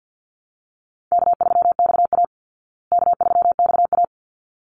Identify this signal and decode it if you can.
Morse code, heard as a keyed tone, about 50 wpm, 700 Hz. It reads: K3XA K3XA
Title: Call Signs Repeated 3x - 50wpm